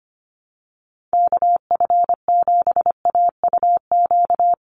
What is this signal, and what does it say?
Morse code, heard as a keyed tone, about 25 wpm, 690 Hz. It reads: KF7AUQ
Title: USA Call Signs 25wpm